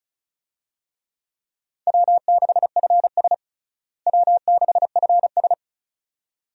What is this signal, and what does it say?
Morse code, heard as a keyed tone, about 34 wpm, 700 Hz. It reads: W6FS W6FS
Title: USA Call Signs - 4 Characters - Repeated 2x - 35wpm